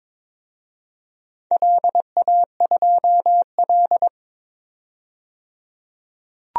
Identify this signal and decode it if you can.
Morse code, heard as a keyed tone, about 22 wpm, 700 Hz. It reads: LA2L
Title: DXCC-100 Call Signs - 4 Characters - 22wpm